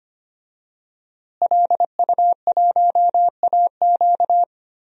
Morse code, heard as a keyed tone, about 25 wpm, 700 Hz. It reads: LU1AQ
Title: DXCC-100 Call Signs - 5 Characters - Repeated 2x - 25wpm